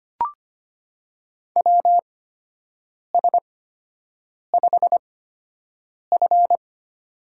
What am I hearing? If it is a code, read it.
Morse code, heard as a keyed tone, about 25 wpm, spaced as for 7 wpm, 700 Hz. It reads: WS5F
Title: USA Call Signs - 4 Characters - 8x Character Spacing - 25wpm